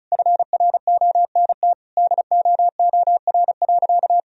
Morse code, heard as a keyed tone, about 36 wpm, 700 Hz. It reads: FRONT DOOR.
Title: Sentences from Top 1000 Words - Review Part 2 - 35wpm